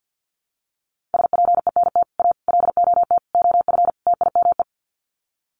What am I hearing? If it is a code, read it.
Morse code, heard as a keyed tone, about 50 wpm, 700 Hz. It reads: SPENT A LOT OF TIME
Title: Sets of 5 Words 50wpm